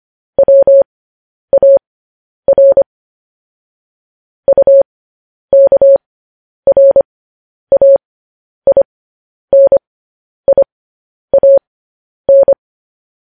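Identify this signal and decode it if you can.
Morse code, heard as a keyed tone, about 25 wpm, spaced as for 10 wpm, 550 Hz. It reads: WAR UKRAINIAN
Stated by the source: Morse code news headlines broadcast at 10 wpm.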